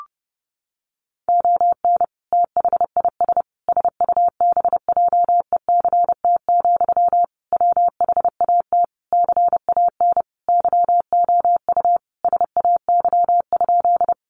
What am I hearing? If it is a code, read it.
Morse code, heard as a keyed tone, about 30 wpm, 700 Hz. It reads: ON THIS SUBJECT, WHAT CAN YOU SAY?
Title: Sentences from Top 800 Words 30wpm